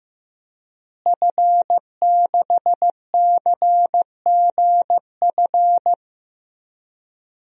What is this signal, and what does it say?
Morse code, heard as a keyed tone, about 15 wpm, 700 Hz. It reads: F6CGF
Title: DXCC-100 Call Signs - 5 Characters - Repeated 2x - 15wpm